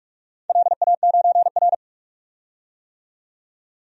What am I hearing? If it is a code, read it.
Morse code, heard as a keyed tone, about 46 wpm, 700 Hz. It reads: LA9R
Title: DXCC-100 Call Signs - 4 Characters - 45wpm